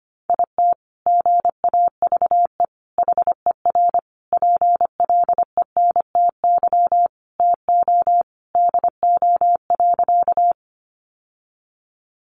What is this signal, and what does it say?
Morse code, heard as a keyed tone, about 25 wpm, 700 Hz. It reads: IT GAVE HER PLENTY TO DO.
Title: Sentences from Top 1300 Words 25wpm